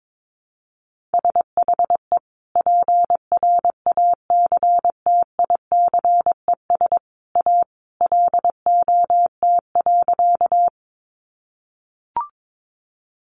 Morse code, heard as a keyed tone, about 22 wpm, 700 Hz. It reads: SHE PRACTICES A LOT.
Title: Additional Easy Phrases 22wpm